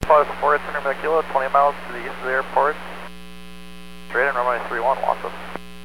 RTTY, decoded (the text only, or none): none